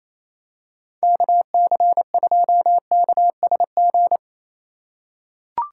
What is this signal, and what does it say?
Morse code, heard as a keyed tone, about 28 wpm, 700 Hz. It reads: KC2KSG
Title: USA Call Signs - 2x Character Spacing - 28wpm